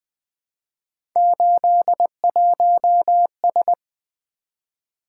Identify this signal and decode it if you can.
Morse code, heard as a keyed tone, about 20 wpm, 700 Hz. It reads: 81S
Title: Three Letter-Number Combinations - 20wpm